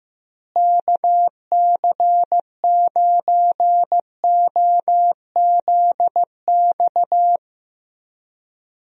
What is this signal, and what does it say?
Morse code, heard as a keyed tone, about 16 wpm, 700 Hz. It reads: KC9OZX
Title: USA Call Signs - 6 Characters - Repeated 2x - 15wpm